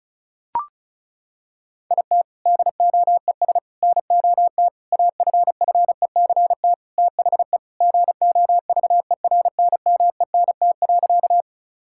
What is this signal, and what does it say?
Morse code, heard as a keyed tone, about 36 wpm, 700 Hz. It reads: IT DOES NOT AFFECT THE GOVERNMENT.
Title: Sentences from Top 1000 Words - Review Part 6 - 35wpm